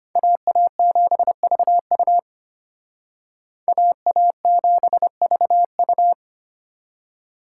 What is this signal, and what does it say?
Morse code, heard as a keyed tone, about 32 wpm, 700 Hz. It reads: AA7VU AA7VU